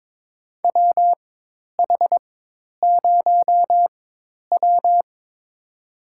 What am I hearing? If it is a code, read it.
Morse code, heard as a keyed tone, about 22 wpm, spaced as for 10 wpm, 700 Hz. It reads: WH0W
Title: USA Call Signs - 4 Characters - 4x Character Spacing - 22wpm